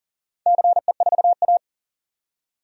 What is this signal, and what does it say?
Morse code, heard as a keyed tone, about 42 wpm, 700 Hz. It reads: CE4A
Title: DXCC-100 Call Signs - 4 Characters - 40wpm